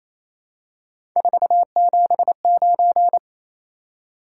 Morse code, heard as a keyed tone, about 28 wpm, 700 Hz. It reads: 479
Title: Top 180 QSO Elements 28wpm